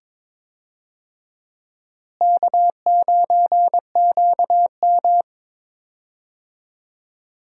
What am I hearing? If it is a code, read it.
Morse code, heard as a keyed tone, about 22 wpm, 700 Hz. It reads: K9QM
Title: USA Call Signs - 4 Characters - Repeated 2x - 22wpm